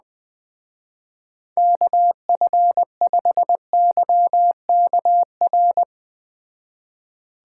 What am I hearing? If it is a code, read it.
Morse code, heard as a keyed tone, about 21 wpm, 700 Hz. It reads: KF5YKR
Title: Speed Racing - USA 2x3 Call Signs 17wpm